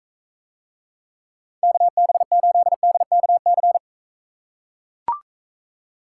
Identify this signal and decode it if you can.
Morse code, heard as a keyed tone, about 44 wpm, 700 Hz. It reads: KB8DKC